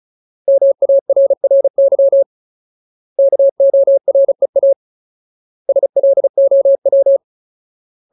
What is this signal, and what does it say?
Morse code, heard as a keyed tone, about 34 wpm, 550 Hz. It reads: MARRY KOREA SLOW
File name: Ultimate Word List / 35WPM Ultimate